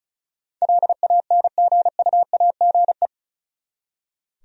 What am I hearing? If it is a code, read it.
Morse code, heard as a keyed tone, about 44 wpm, 700 Hz. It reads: LANGUAGE
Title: Speed-Racing Top 500 Words 35wpm